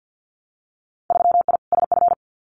Morse code, heard as a keyed tone, about 50 wpm, 700 Hz. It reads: 3I SF
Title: Winter Field Day Exchanges 50wpm